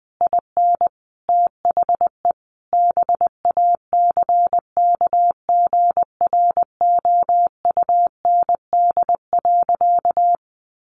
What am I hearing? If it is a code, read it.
Morse code, heard as a keyed tone, about 20 wpm, 700 Hz. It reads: IN THE BACKGROUND.
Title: Sentences from Top 1300 Words 20wpm